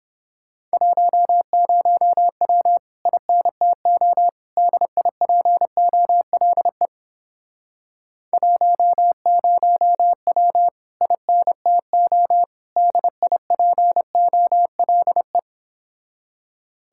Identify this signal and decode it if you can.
Morse code, heard as a keyed tone, about 29 wpm, 700 Hz. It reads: 10W INTO DIPOLE 10W INTO DIPOLE